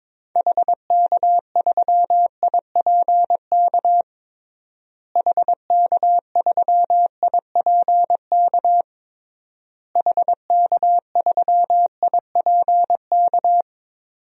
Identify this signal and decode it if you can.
Morse code, heard as a keyed tone, about 22 wpm, 700 Hz. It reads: HK3IPK HK3IPK HK3IPK
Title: DXCC-100 Call Signs - 6 Characters - Repeated 3x - 22wpm